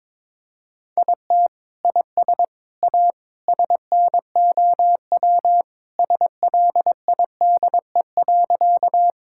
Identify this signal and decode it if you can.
Morse code, heard as a keyed tone, about 21 wpm, 700 Hz. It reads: IT IS A SNOW SLIDE.
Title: Sentences from Top 1600 Words 22wpm